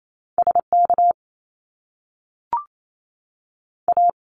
Morse code, heard as a keyed tone, about 28 wpm, spaced as for 6 wpm, 700 Hz. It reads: SKA
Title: Canadian Provinces and Territories 28wpm